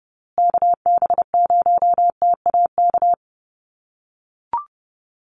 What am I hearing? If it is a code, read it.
Morse code, heard as a keyed tone, about 30 wpm, 700 Hz. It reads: KB0TAK
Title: USA Call Signs Repeated 3x 30wpm